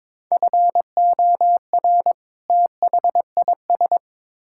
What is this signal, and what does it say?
Morse code, heard as a keyed tone, about 22 wpm, 700 Hz. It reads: FOR THIS
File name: Build A Phrase - 2-Word Phrases - 22wpm